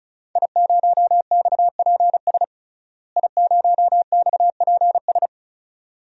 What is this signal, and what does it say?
Morse code, heard as a keyed tone, about 34 wpm, 700 Hz. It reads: I0XPS I0XPS
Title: DXCC-100 Call Signs - 5 Characters - Repeated 2x - 35wpm